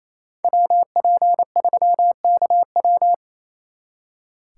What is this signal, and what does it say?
Morse code, heard as a keyed tone, about 33 wpm, 700 Hz. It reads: WP3KW
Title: Speed Racing - USA 2x2 Call Signs 28wpm